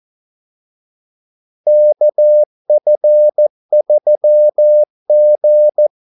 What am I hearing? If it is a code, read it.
Morse code, heard as a keyed tone, about 14 wpm, 600 Hz. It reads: KF3G
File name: Callsigns-14wpm